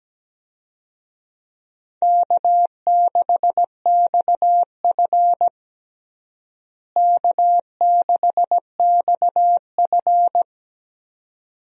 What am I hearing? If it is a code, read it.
Morse code, heard as a keyed tone, about 17 wpm, 700 Hz. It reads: K6XF K6XF
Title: USA Call Signs - 4 Characters - Repeated 2x - 17wpm